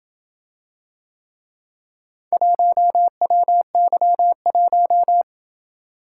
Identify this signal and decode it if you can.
Morse code, heard as a keyed tone, about 26 wpm, 700 Hz. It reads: 1WY1